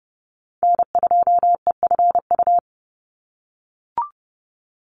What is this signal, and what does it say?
Morse code, heard as a keyed tone, about 30 wpm, 700 Hz. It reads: N2EFU
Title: USA Call Signs 30wpm